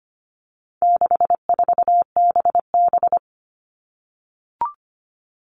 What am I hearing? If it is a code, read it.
Morse code, heard as a keyed tone, about 25 wpm, 700 Hz. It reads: 64BB
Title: All Cut Numbers - 4-Digit 25wpm